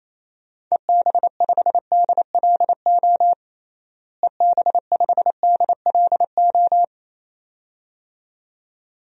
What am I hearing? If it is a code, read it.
Morse code, heard as a keyed tone, about 28 wpm, 700 Hz. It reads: EB5DLO EB5DLO
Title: DXCC-100 Call Signs - 6 Characters - Repeated 2x - 28wpm